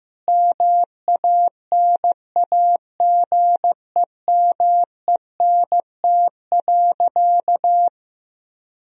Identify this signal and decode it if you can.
Morse code, heard as a keyed tone, about 15 wpm, 700 Hz. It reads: MANAGEMENT.
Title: Sentences from Top 1000 Words - Review Part 4 - 15wpm